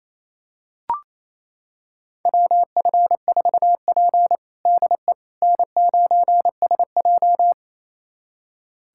Morse code, heard as a keyed tone, about 28 wpm, 700 Hz. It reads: WF4P DE N9SJ
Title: USA Callsign DE Callsign - 4 Characters - 28wpm